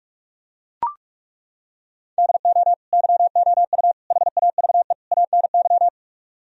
Morse code, heard as a keyed tone, about 38 wpm, 700 Hz. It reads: DO YOU HAVE ANY